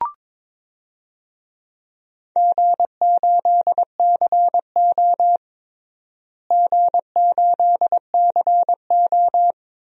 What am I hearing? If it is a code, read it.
Morse code, heard as a keyed tone, about 22 wpm, 700 Hz. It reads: G8CO G8CO
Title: DXCC-100 Call Signs - 4 Characters - Repeated 2x - 22wpm